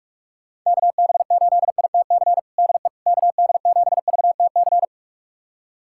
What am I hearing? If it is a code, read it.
Morse code, heard as a keyed tone, about 46 wpm, 700 Hz. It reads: KB8ITC DE KD7VTC
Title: USA Callsign DE Callsign - 6 Characters - 45wpm